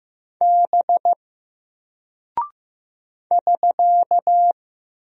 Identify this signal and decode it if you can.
Morse code, heard as a keyed tone, about 15 wpm, 700 Hz. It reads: B <SK>
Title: Single Character (Letter:Number:Punctuation:Prosign) Alt 15wpm